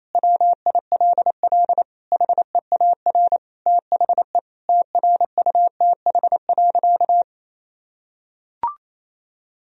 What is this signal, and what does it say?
Morse code, heard as a keyed tone, about 28 wpm, 700 Hz. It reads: WILL HEAR THE TRUTH.
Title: Sentences from Top 1000 Words - Review Part 1 - 28wpm